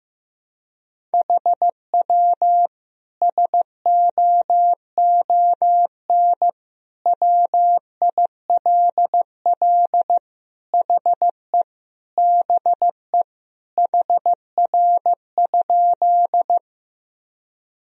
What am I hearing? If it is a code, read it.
Morse code, heard as a keyed tone, about 15 wpm, 700 Hz. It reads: HW SOON WILL HE BE HR?
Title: Sentences from Top 400 Words and Abbreviations 15wpm